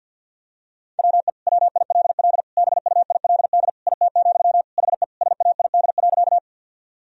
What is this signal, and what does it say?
Morse code, heard as a keyed tone, about 51 wpm, 700 Hz. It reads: WE WILL BUILD IT, HE SAID.
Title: Sentences from Top 1000 Words - Review Part 1 - 50wpm